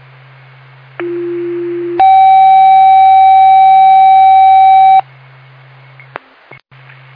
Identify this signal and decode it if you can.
Two-tone page, heard as a one-second tone, then a three-second tone, about 350 then 770 Hz. It is heard from 1.0 to 5.0 s.